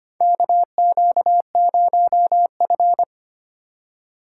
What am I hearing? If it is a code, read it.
Morse code, heard as a keyed tone, about 25 wpm, 700 Hz. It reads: KQ0F
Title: USA Call Signs - 4 Characters - 2x Character Spacing - 25wpm